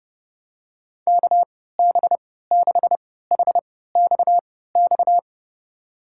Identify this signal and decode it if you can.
Morse code, heard as a keyed tone, about 30 wpm, spaced as for 17 wpm, 700 Hz. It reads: KB6HXX
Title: USA Call Signs - 6 Characters - 3x Character Spacing - 30wpm